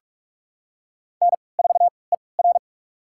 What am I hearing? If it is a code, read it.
Morse code, heard as a keyed tone, about 46 wpm, spaced as for 26 wpm, 700 Hz. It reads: N4ER
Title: USA Call Signs - 4 Characters - 3x Character Spacing - 45wpm